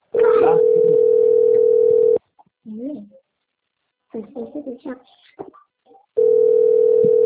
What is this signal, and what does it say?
Ringback tone (North America), 440 + 480 Hz, with speech.
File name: rg-1600-unknown- - - .6772